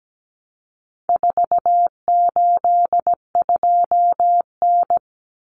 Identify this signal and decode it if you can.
Morse code, heard as a keyed tone, about 17 wpm, 700 Hz. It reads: 482N
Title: Common Cut Numbers - 4-Digit 17wpm